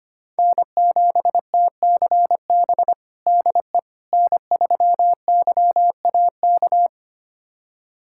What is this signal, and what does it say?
Morse code, heard as a keyed tone, about 25 wpm, 700 Hz. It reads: N7TCB DE N3YAK
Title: USA Callsign DE Callsign - 5 Characters - 25wpm